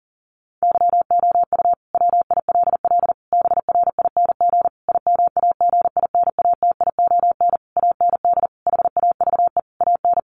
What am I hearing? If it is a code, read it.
Morse code, heard as a keyed tone, about 40 wpm, 700 Hz. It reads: YOU WILL BRING IMAGINATION AND HAVE AN
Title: Sentences from Top 1300 Words 40wpm